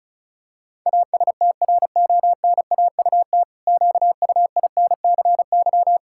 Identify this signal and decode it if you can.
Morse code, heard as a keyed tone, about 36 wpm, 700 Hz. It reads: ASTRONAUT QUINCY
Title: Pangram Challenge v2 - 35wpm